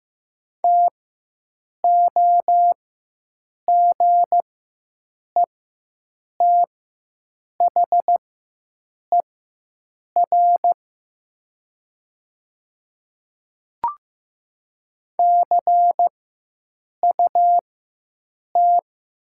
Morse code, heard as a keyed tone, about 15 wpm, spaced as for 7 wpm, 700 Hz. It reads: TOGETHER CUT
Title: 4x Character Spacing - Top 300 Words - 15wpm